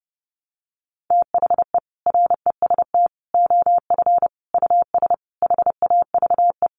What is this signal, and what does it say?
Morse code, heard as a keyed tone, about 30 wpm, 700 Hz. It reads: THE REST OF US HAVE
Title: Sentences from Top 700 Words 30wpm